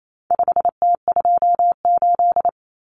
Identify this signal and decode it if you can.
Morse code, heard as a keyed tone, about 28 wpm, 700 Hz. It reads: 5T28
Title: Common Cut Numbers - 4-Digit 28wpm